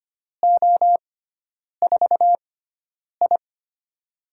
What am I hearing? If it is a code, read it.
Morse code, heard as a keyed tone, about 25 wpm, spaced as for 9 wpm, 700 Hz. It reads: O4I
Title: USA Call Signs - 4 Characters - 6x Character Spacing - 25wpm